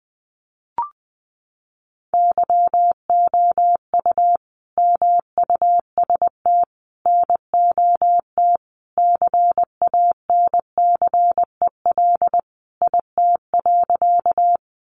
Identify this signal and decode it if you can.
Morse code, heard as a keyed tone, about 19 wpm, 700 Hz. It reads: YOU MUST NOT CANCEL IT.
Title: Sentences from Top 1200 Words 20wpm